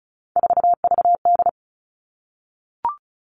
Morse code, heard as a keyed tone, about 34 wpm, 700 Hz. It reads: 4VD
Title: All Cut Numbers - 3-Digit 35wpm